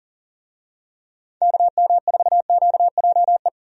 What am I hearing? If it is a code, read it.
Morse code, heard as a keyed tone, about 37 wpm, 700 Hz. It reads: KM4QJE
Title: USA Call Signs - 6 Characters - 40wpm